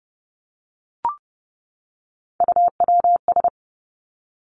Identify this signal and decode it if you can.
Morse code, heard as a keyed tone, about 30 wpm, 700 Hz. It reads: UWS
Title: Three Challenging Letter-Number Combinations 30wpm